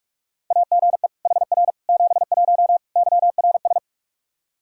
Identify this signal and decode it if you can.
Morse code, heard as a keyed tone, about 46 wpm, 700 Hz. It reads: AGE HR 71 YRS
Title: Practice QSO - Exchange 3 - More Info - 45wpm